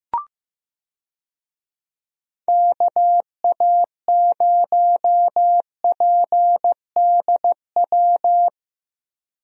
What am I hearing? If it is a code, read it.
Morse code, heard as a keyed tone, about 16 wpm, 700 Hz. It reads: KA0PDW
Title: USA Call Signs - 6 Characters - Repeated 3x - 15wpm